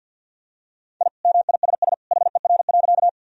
Morse code, heard as a keyed tone, about 52 wpm, 700 Hz. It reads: I MISS HER.
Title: Sentences from Top 1000 Words - Review Part 3 - 50wpm